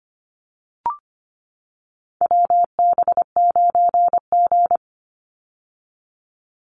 Morse code, heard as a keyed tone, about 25 wpm, 700 Hz. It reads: WB9G
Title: Call Signs - 25wpm